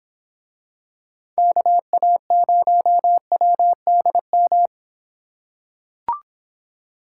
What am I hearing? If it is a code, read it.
Morse code, heard as a keyed tone, about 24 wpm, 700 Hz. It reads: KA0WDM